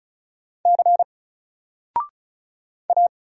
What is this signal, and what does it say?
Morse code, heard as a keyed tone, about 35 wpm, 700 Hz. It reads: C A
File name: Single-Letter-Number-35wpm